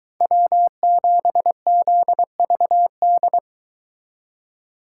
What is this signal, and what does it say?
Morse code, heard as a keyed tone, about 23 wpm, 700 Hz. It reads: W7ZVD